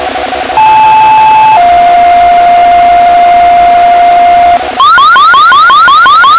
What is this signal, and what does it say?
Two-tone page, heard as a one-second tone, then a three-second tone, about 870 then 700 Hz, with noise. It is heard from 0.6 to 4.6 s.